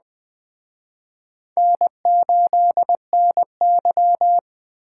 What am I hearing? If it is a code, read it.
Morse code, heard as a keyed tone, about 20 wpm, 700 Hz. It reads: N8NY
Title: Speed Racing - USA 1x2 Call Signs 17wpm